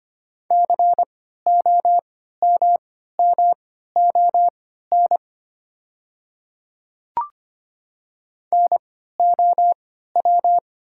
Morse code, heard as a keyed tone, about 25 wpm, spaced as for 14 wpm, 700 Hz. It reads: COMMON NOW
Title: 3x Character Spacing - Top 500 Words - 25wpm